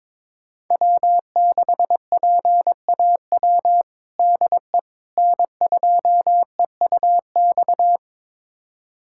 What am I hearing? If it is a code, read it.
Morse code, heard as a keyed tone, about 22 wpm, 700 Hz. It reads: W6PAW DE N2EUX
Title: USA Callsign DE Callsign - 5 Characters - 22wpm